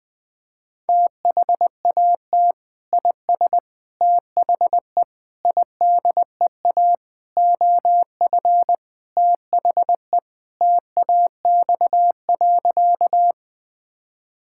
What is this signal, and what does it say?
Morse code, heard as a keyed tone, about 20 wpm, 700 Hz. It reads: THAT IS THE IDEA OF THE TAX.
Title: Sentences from Top 1000 Words - Review Part 5 - 20wpm